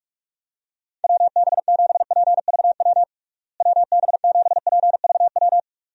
Morse code, heard as a keyed tone, about 46 wpm, 700 Hz. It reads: WB7PVW WB7PVW
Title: USA Call Signs - 6 Characters - Repeated 2x - 45wpm